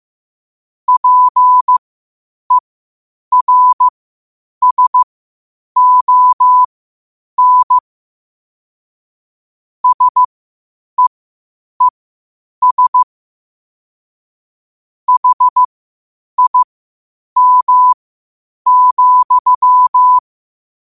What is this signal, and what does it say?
Morse code, heard as a keyed tone, about 15 wpm, spaced as for 8 wpm, 1000 Hz. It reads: PERSON SEES HIM,
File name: qotdfiles / 8wpm